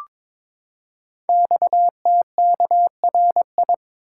Morse code, heard as a keyed tone, about 23 wpm, 700 Hz. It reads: XTKRI
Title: Code Groups - 5 Letters - 22wpm